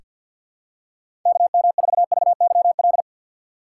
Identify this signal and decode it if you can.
Morse code, heard as a keyed tone, about 46 wpm, 690 Hz. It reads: KM4VYL
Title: USA Call Signs - 6 Characters - 3x Character Spacing - 50wpm